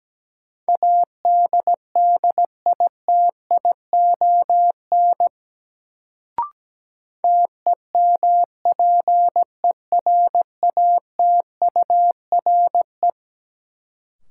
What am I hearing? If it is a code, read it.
Morse code, heard as a keyed tone, about 17 wpm, 700 Hz. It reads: ADDITION TEMPERATURE
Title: Sentences from Top 1500 Words - New Word Drill Alt - 17wpm